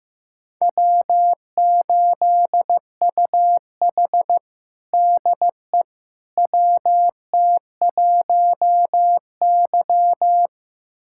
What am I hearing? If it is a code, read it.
Morse code, heard as a keyed tone, about 15 wpm, 700 Hz. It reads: W8UH DE WT1Y
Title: USA Callsign DE Callsign - 4 Characters - 15wpm